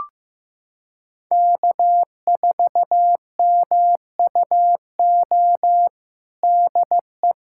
Morse code, heard as a keyed tone, about 15 wpm, 700 Hz. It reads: K4MUO DE
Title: USA Callsign DE Callsign - 5 Characters - 15wpm